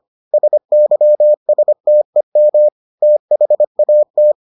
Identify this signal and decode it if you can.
Morse code, heard as a keyed tone, about 25 wpm, 600 Hz. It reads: SYSTEM THAT